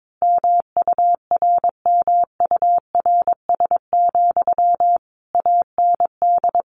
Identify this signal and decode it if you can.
Morse code, heard as a keyed tone, about 22 wpm, 700 Hz. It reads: MURMURS, AND
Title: A Sound Of Thunder 22wpm